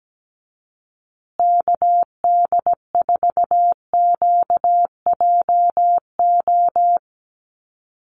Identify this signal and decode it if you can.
Morse code, heard as a keyed tone, about 17 wpm, 700 Hz. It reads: KD4QJO
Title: USA Call Signs Repeated 2x 17wpm